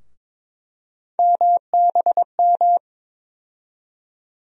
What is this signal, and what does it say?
Morse code, heard as a keyed tone, about 22 wpm, 700 Hz. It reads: MBM
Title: USA Call Sign Postfixes - 3 Letters 22wpm